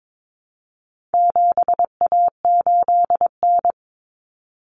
Morse code, heard as a keyed tone, about 22 wpm, 700 Hz. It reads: 7A8N
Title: Common Cut Numbers - 4-Digit 22wpm